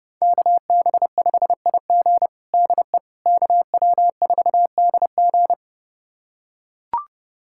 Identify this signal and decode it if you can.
Morse code, heard as a keyed tone, about 30 wpm, 700 Hz. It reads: KB5IG DE KW4DG
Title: USA Callsign DE Callsign - 5 Characters - 30wpm